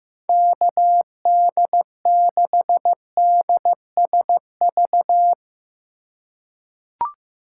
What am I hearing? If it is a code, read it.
Morse code, heard as a keyed tone, about 16 wpm, 700 Hz. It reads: KD6DSV
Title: USA Call Signs - 3x Character Spacing - 15wpm